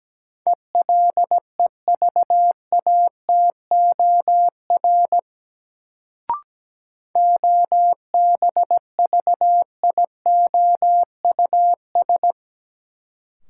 Morse code, heard as a keyed tone, about 17 wpm, 700 Hz. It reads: ELEVATOR OBVIOUS
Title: Sentences from Top 1600 Words - New Word Drill Alt - 17wpm